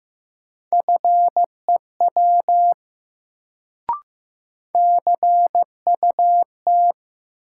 Morse code, heard as a keyed tone, about 15 wpm, 700 Hz. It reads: FEW CUT
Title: Sentences from Top 300 Words - New Word Drill - 15wpm